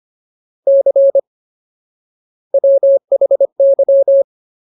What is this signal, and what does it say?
Morse code, heard as a keyed tone, about 25 wpm, 550 Hz. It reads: C WHY
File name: Ultimate Word List / 25WPM Ultimate